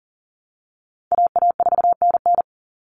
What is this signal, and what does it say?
Morse code, heard as a keyed tone, about 40 wpm, 700 Hz. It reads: AA4NN
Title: Call Signs Repeated 3x - 40wpm